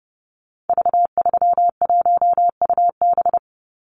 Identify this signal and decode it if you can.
Morse code, heard as a keyed tone, about 30 wpm, 700 Hz. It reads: V31UB
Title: Call Signs - 30wpm